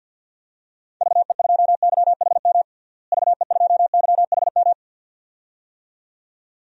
Morse code, heard as a keyed tone, about 52 wpm, 700 Hz. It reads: VE2YHM VE2YHM
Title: DXCC-100 Call Signs - 6 Characters - Repeated 2x - 50wpm